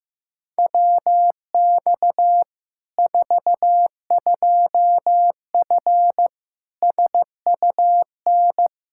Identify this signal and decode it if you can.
Morse code, heard as a keyed tone, about 15 wpm, 700 Hz. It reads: WX 42F SUN
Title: K1USN SST QSO Practice 15wpm